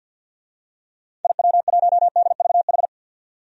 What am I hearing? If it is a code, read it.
Morse code, heard as a keyed tone, about 53 wpm, 700 Hz. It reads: IW1DVH
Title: DXCC-100 Call Signs - 6 Characters - 50wpm